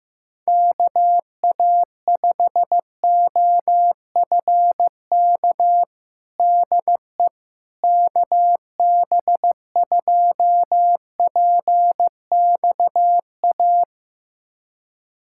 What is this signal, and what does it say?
Morse code, heard as a keyed tone, about 15 wpm, 700 Hz. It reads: KA5OFK DE KB2PXA
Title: USA Callsign DE Callsign - 6 Characters - 15wpm